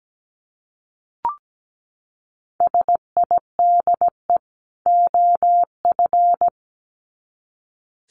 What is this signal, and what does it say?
Morse code, heard as a keyed tone, about 17 wpm, 700 Hz. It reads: SIDE OF
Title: Sets of 2 Words 17wpm